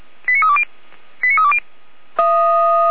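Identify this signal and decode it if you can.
Five-tone selective call (ZVEI), heard as quick runs of five tones, tones of about 70 ms. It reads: 89320 89320
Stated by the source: ZVEI five-tone selective call received by SDR, address 89014.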